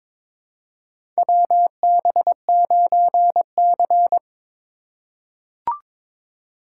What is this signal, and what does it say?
Morse code, heard as a keyed tone, about 22 wpm, 700 Hz. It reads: WB9C
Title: USA Call Signs - 4 Characters - 3x Character Spacing - 22wpm